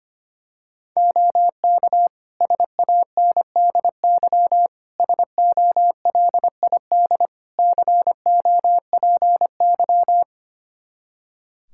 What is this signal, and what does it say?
Morse code, heard as a keyed tone, about 25 wpm, 700 Hz. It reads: OK SANDY SOLID COPY
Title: K1USN SST QSO Practice 25wpm